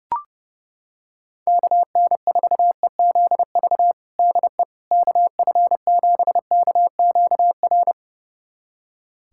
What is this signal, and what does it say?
Morse code, heard as a keyed tone, about 32 wpm, 700 Hz. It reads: KN4EZV DE KF7KQR
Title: USA Callsign DE Callsign 30wpm